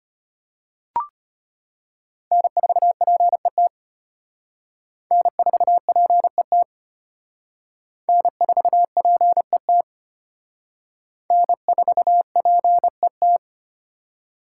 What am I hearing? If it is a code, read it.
Morse code, heard as a keyed tone, about 31 wpm, 700 Hz. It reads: N4PET N4PET N4PET N4PET